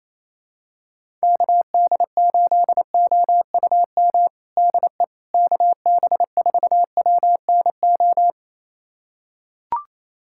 Morse code, heard as a keyed tone, about 28 wpm, 700 Hz. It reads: KD8OUM DE KB4WNO
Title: USA Callsign DE Callsign 28wpm